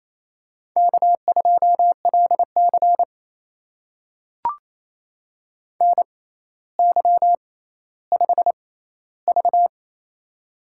Morse code, heard as a keyed tone, about 28 wpm, spaced as for 14 wpm, 700 Hz. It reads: K2LC NY5V
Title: USA Call Signs - 4 Characters - 6x Character Spacing - 28wpm